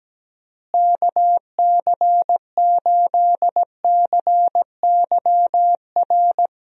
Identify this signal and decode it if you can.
Morse code, heard as a keyed tone, about 17 wpm, 700 Hz. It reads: KC8CYR
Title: USA Call Signs - 6 Characters - Repeated 3x - 17wpm